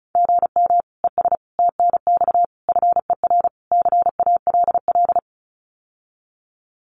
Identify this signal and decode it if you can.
Morse code, heard as a keyed tone, about 36 wpm, 700 Hz. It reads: GM ES TNX FER CALL
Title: Common QSO Phrases Encore 35wpm